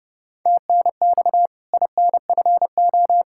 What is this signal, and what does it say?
Morse code, heard as a keyed tone, about 28 wpm, 700 Hz. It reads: TNX INFO
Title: K1USN SST QSO Practice 30wpm